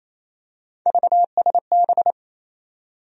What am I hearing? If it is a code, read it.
Morse code, heard as a keyed tone, about 27 wpm, 700 Hz. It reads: VSB
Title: USA Call Sign Postfixes - 3 Letters 28wpm